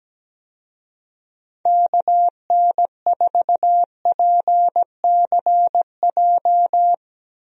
Morse code, heard as a keyed tone, about 17 wpm, 700 Hz. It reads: KN4PCJ
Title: USA Call Signs - 6 Characters - Repeated 3x - 17wpm